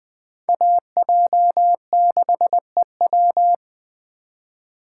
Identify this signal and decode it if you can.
Morse code, heard as a keyed tone, about 21 wpm, 700 Hz. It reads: AJ6EW